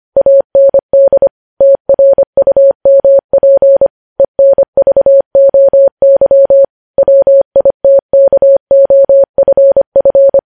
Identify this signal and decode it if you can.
Morse code, heard as a keyed tone, about 25 wpm, 550 Hz. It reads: AND TRUMP ENVOY WITKOFF
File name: podcast / News headlines-1x-25x25